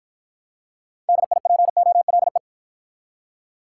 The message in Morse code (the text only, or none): DIPOLE